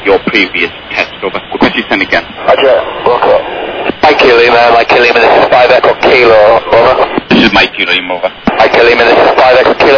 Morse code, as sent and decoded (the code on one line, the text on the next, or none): none